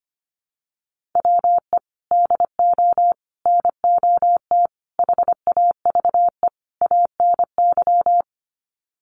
Morse code, heard as a keyed tone, about 25 wpm, 700 Hz. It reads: WE DO NOT HAVE ANY
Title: Sets of 5 Words 25wpm